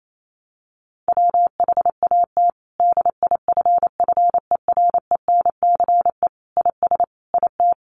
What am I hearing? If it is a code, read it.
Morse code, heard as a keyed tone, about 28 wpm, 700 Hz. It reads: WHAT DIFFERENCE IS IT
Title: Sentences from Top 800 Words 28wpm